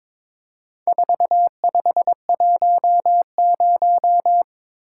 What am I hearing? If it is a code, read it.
Morse code, heard as a keyed tone, about 22 wpm, 700 Hz. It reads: 4510
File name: practice / 22wpm / Four Letter-Number Combinations - 22wpm